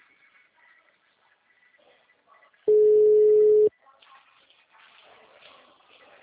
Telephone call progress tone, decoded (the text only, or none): ringback tone